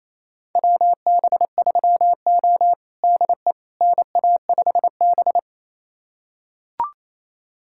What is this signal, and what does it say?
Morse code, heard as a keyed tone, about 28 wpm, 700 Hz. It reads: WB3O DE NA5B
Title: USA Callsign DE Callsign - 4 Characters - 28wpm